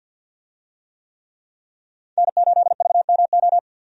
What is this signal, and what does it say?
Morse code, heard as a keyed tone, about 50 wpm, 700 Hz. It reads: N8VMO
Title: USA Call Signs - 5 Characters - Repeated 3x - 50wpm